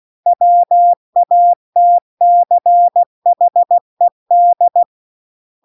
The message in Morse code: WATCHED

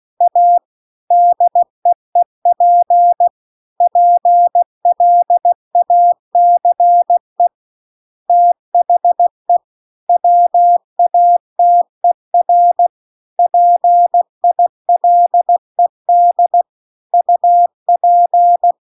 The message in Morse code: A DEEP PLACE THE WATER PILED UP

